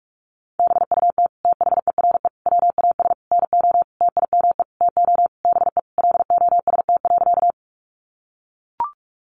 Morse code, heard as a keyed tone, about 44 wpm, 700 Hz. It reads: BUT THERE WAS NO TIME TO BE LOST.
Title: Sentences from Top 1400 Words 45wpm